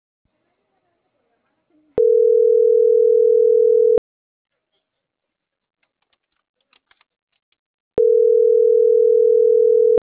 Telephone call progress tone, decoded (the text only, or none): ringback tone